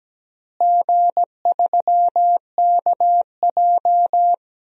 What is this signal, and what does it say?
Morse code, heard as a keyed tone, about 17 wpm, 700 Hz. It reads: G3KJ
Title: DXCC-100 Call Signs - 4 Characters - Repeated 2x - 17wpm